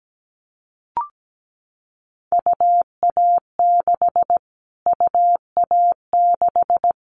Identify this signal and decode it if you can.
Morse code, heard as a keyed tone, about 17 wpm, 700 Hz. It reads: UA6 UA6
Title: Callsign Prefixes for Top 25 DXCC - Repeated 3x - 17wpm